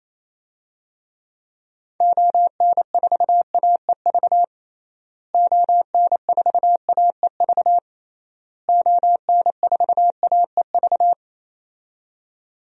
Morse code, heard as a keyed tone, about 28 wpm, 700 Hz. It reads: ON4AEV ON4AEV ON4AEV
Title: DXCC-100 Call Signs - 6 Characters - Repeated 3x - 28wpm